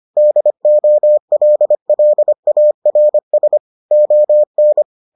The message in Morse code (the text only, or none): DOLLARS ON